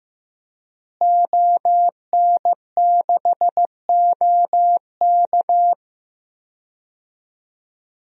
Morse code, heard as a keyed tone, about 15 wpm, 700 Hz. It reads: ON6OK
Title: DXCC-100 Call Signs - 5 Characters - Repeated 2x - 15wpm